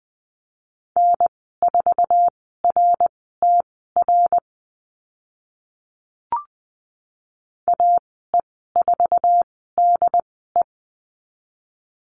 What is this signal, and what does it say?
Morse code, heard as a keyed tone, about 20 wpm, spaced as for 15 wpm, 700 Hz. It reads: N4RTR AE4DE
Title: 076 Lesson 13g - Letter L - Call Signs 20wpm